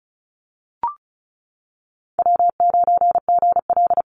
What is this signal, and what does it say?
Morse code, heard as a keyed tone, about 34 wpm, 700 Hz. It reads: W9GL
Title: Call Signs - 35wpm